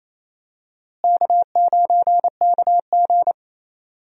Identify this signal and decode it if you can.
Morse code, heard as a keyed tone, about 28 wpm, 700 Hz. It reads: K9KG
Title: USA Call Signs - 4 Characters - Repeated 3x - 28wpm